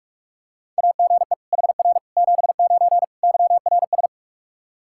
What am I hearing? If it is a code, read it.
Morse code, heard as a keyed tone, about 46 wpm, 700 Hz. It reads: AGE HR 79 YRS
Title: Practice QSO - Exchange 3 - More Info - 45wpm